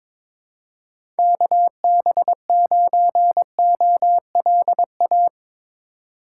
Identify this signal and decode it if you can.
Morse code, heard as a keyed tone, about 23 wpm, 700 Hz. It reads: KB9OLA
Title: USA Call Signs - 8x Character Spacing - 22wpm